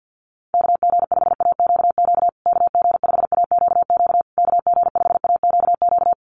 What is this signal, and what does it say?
Morse code, heard as a keyed tone, about 50 wpm, 700 Hz. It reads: KG5AQQ KG5AQQ KG5AQQ
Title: USA Call Signs Repeated 3x 50wpm